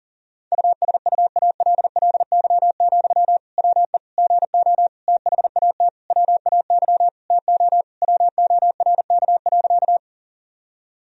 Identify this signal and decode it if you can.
Morse code, heard as a keyed tone, about 40 wpm, 700 Hz. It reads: USUALLY, WE GO THAT WAY TO WORK.
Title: Sentences from Top 1000 Words - Review Part 3 - 40wpm